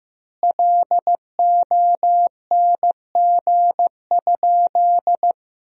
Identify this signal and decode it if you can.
Morse code, heard as a keyed tone, about 15 wpm, 700 Hz. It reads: LONG?
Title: 4x Word-Spacing - Sentences from Top 500 Words - 15wpm